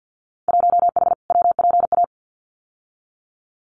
Morse code, heard as a keyed tone, about 48 wpm, 700 Hz. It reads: 1H WPA
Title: Winter Field Day Exchanges 50wpm